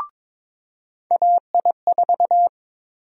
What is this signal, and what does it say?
Morse code, heard as a keyed tone, about 22 wpm, 700 Hz. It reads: AI4
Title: USA Call Sign Prefixes - 22wpm